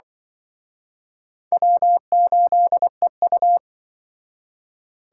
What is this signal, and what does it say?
Morse code, heard as a keyed tone, about 24 wpm, 700 Hz. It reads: W8EU
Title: Speed Racing - USA Call Signs 20wpm